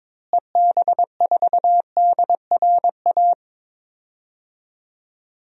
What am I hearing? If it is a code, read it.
Morse code, heard as a keyed tone, about 22 wpm, 700 Hz. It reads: EB4DRA
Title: DXCC-100 Call Signs - 6 Characters - Repeated 3x - 22wpm